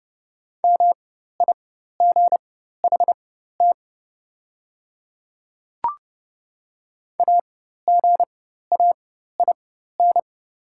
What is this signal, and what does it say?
Morse code, heard as a keyed tone, about 30 wpm, spaced as for 14 wpm, 700 Hz. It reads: MIGHT AGAIN
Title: 4x Character Spacing - Top 500 Words - 30wpm